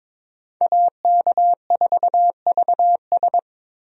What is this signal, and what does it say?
Morse code, heard as a keyed tone, about 22 wpm, 700 Hz. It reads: AK4VS
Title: USA Call Signs - 5 Characters - Repeated 3x - 22wpm